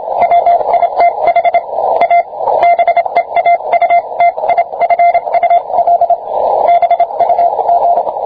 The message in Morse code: WITH A BEAUTIFUL BUG